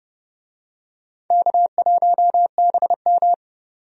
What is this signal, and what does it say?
Morse code, heard as a keyed tone, about 31 wpm, 700 Hz. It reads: K1BM
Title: USA Call Signs - 4 Characters - 30wpm